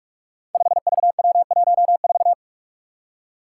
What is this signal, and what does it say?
Morse code, heard as a keyed tone, about 44 wpm, 700 Hz. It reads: 5VW14
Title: Five Challenging Letter-Number Combinations 45wpm